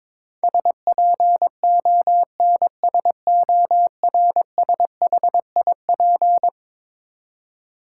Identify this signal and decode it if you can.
Morse code, heard as a keyed tone, about 22 wpm, 700 Hz. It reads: SPONSORSHIP
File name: practice / 22wpm / Eleven Letter Words - 22wpm